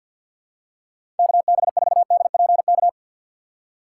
Morse code, heard as a keyed tone, about 48 wpm, 700 Hz. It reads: KB4DPK
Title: USA Call Signs - 2x Character Spacing - 50wpm